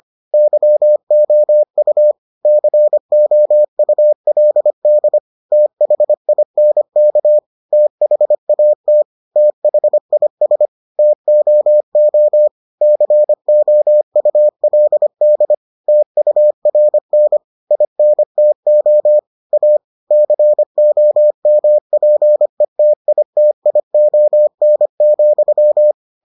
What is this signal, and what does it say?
Morse code, heard as a keyed tone, about 25 wpm, 600 Hz. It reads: YOU COULD THINK THAT THIS TOO COULD TURN INTO A COMPETITION,